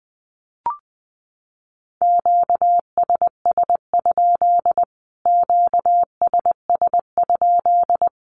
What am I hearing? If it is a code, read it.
Morse code, heard as a keyed tone, about 20 wpm, 700 Hz. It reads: QSS? QSS?
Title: Q-Codes Repeated 3x 20wpm